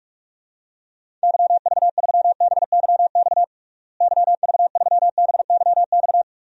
Morse code, heard as a keyed tone, about 43 wpm, 700 Hz. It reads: YV3BYX YV3BYX
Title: DXCC-100 Call Signs - 6 Characters - Repeated 2x - 45wpm